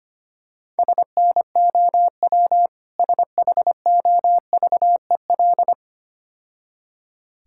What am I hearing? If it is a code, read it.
Morse code, heard as a keyed tone, about 26 wpm, 700 Hz. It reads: SNOW SHOVEL
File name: Easy Words - Speed-Racing with Hints - 25wpm